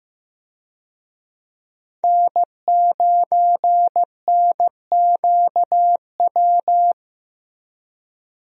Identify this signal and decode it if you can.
Morse code, heard as a keyed tone, about 16 wpm, 700 Hz. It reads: N9NQW
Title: USA Call Signs - 5 Characters - Repeated 3x - 15wpm